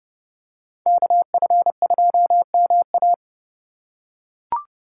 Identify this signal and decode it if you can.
Morse code, heard as a keyed tone, about 30 wpm, 700 Hz. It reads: KF2MA